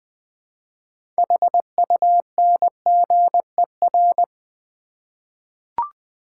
Morse code, heard as a keyed tone, about 20 wpm, 700 Hz. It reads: HUNGER
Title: 6 Letter Words - First 3 Letters Given - 20wpm